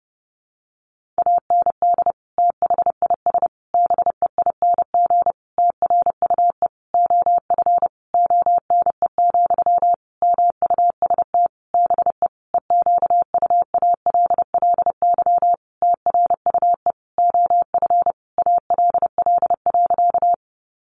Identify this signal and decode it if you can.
Morse code, heard as a keyed tone, about 30 wpm, 700 Hz. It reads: AND THIS BEING TRUE OF ONE, MUST BE EQUALLY TRUE OF ALL.
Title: Sentences from Top 1400 Words 30wpm